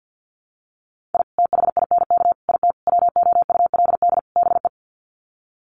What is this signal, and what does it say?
Morse code, heard as a keyed tone, about 48 wpm, 700 Hz. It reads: I THINK IT WOULD BE
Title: Sets of 5 Words 50wpm